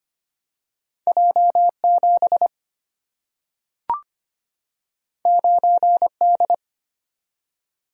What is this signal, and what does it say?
Morse code, heard as a keyed tone, about 25 wpm, 700 Hz. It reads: J7 9D
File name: Two Letter-Number Combinations - 25wpm